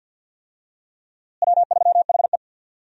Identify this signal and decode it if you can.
Morse code, heard as a keyed tone, about 53 wpm, 700 Hz. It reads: W3HE
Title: USA Call Signs - 4 Characters - Repeated 3x - 50wpm